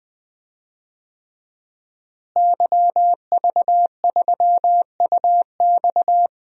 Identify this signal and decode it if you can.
Morse code, heard as a keyed tone, about 20 wpm, 700 Hz. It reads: YV3UX
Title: DXCC-100 Call Signs - 5 Characters - Repeated 3x - 20wpm